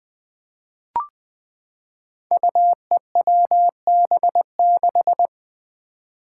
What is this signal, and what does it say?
Morse code, heard as a keyed tone, about 20 wpm, 700 Hz. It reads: UEWB6
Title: Five Challenging Letter-Number Combinations 20wpm